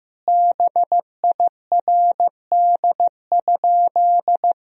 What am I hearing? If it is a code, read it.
Morse code, heard as a keyed tone, about 15 wpm, 700 Hz. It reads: BIRD?
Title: Sentences from Top 1000 Words - Review Part 2 - 15wpm